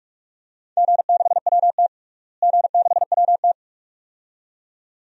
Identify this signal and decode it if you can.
Morse code, heard as a keyed tone, about 46 wpm, 700 Hz. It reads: G6WT G6WT
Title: DXCC-100 Call Signs - 4 Characters - Repeated 2x - 45wpm